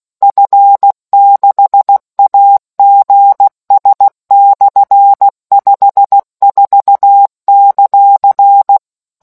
Morse code, heard as a keyed tone, about 16 wpm, 800 Hz. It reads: F6AGS/54;